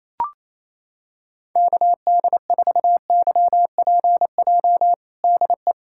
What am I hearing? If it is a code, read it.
Morse code, heard as a keyed tone, about 28 wpm, 700 Hz. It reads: KD4YPJ DE
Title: USA Callsign DE Callsign - 6 Characters - 28wpm